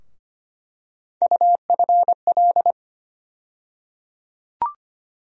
Morse code, heard as a keyed tone, about 25 wpm, 700 Hz. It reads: UFL
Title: USA Call Sign Postfixes - 3 Letters 25wpm